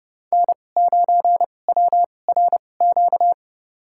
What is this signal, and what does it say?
Morse code, heard as a keyed tone, about 30 wpm, spaced as for 22 wpm, 700 Hz. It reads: N9WRQ
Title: USA Call Signs - 5 Characters - 2x Character Spacing - 30wpm